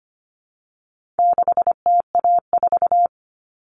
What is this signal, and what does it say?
Morse code, heard as a keyed tone, about 25 wpm, 700 Hz. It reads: 6TA4
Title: All Cut Numbers - 4-Digit 25wpm